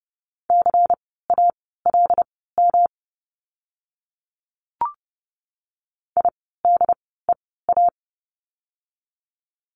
Morse code, heard as a keyed tone, about 30 wpm, spaced as for 17 wpm, 700 Hz. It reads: CALM IDEA
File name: course / 30:10wpm / 108 Lesson 18f - Letter m - Words 30wpm